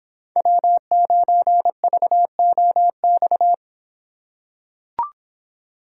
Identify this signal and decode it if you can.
Morse code, heard as a keyed tone, about 25 wpm, 700 Hz. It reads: W9VOX